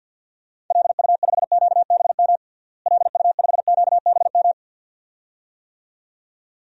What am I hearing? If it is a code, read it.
Morse code, heard as a keyed tone, about 53 wpm, 700 Hz. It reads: LU5QBM LU5QBM
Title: DXCC-100 Call Signs - 6 Characters - Repeated 2x - 50wpm